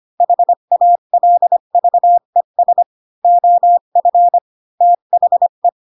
Morse code, heard as a keyed tone, about 25 wpm, 700 Hz. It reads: HALVES OF THE
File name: Pinocchio / Pinocchio ch5 20 25 ik1hge